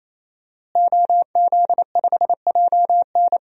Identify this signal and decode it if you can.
Morse code, heard as a keyed tone, about 28 wpm, 700 Hz. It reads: OZ5JN
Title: DXCC-100 Call Signs - 5 Characters - Repeated 3x - 28wpm